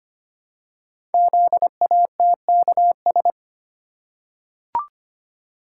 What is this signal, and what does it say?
Morse code, heard as a keyed tone, about 25 wpm, 700 Hz. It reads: ZATKS
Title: Code Groups - 5 Letters - 25wpm